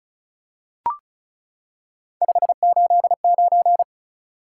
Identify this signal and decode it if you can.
Morse code, heard as a keyed tone, about 36 wpm, 700 Hz. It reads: 589
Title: Top 180 QSO Elements 35wpm